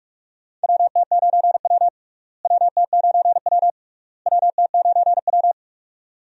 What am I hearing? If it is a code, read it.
Morse code, heard as a keyed tone, about 46 wpm, 700 Hz. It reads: WT9W WT9W WT9W
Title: USA Call Signs - 4 Characters - Repeated 3x - 45wpm